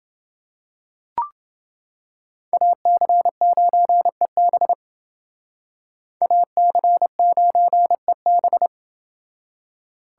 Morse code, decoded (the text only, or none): AC9EB AC9EB